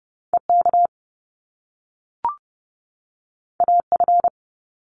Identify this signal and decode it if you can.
Morse code, heard as a keyed tone, about 30 wpm, 700 Hz. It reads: EK AF
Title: Top Two Letter Combinations in Words 30wpm